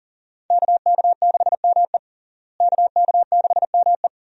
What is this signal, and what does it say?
Morse code, heard as a keyed tone, about 40 wpm, 700 Hz. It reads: KK6ME KK6ME
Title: USA Call Signs - 5 Characters - Repeated 2x - 40wpm